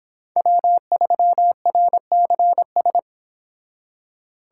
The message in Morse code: W3RCS

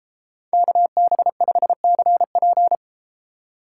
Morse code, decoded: KB5CP

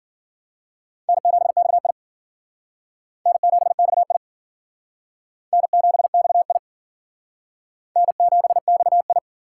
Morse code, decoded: N7XI N7XI N7XI N7XI

